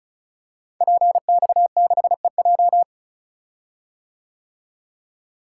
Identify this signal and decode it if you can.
Morse code, heard as a keyed tone, about 36 wpm, 690 Hz. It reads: PX6EJ
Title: Code Groups - 5 Letter:Numbers - 35wpm